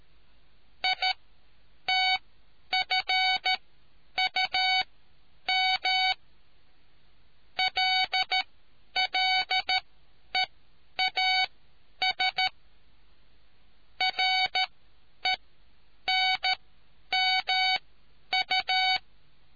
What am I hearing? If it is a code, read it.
Morse code, heard as a keyed tone, about 13 wpm, spaced as for 9 wpm, 750 Hz. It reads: ITFUM LLEAS RENMU